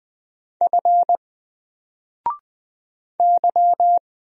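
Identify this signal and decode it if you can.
Morse code, heard as a keyed tone, about 20 wpm, 700 Hz. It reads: F Y
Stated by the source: Single letter and number Morse drill at 20 wpm, from N4LCD.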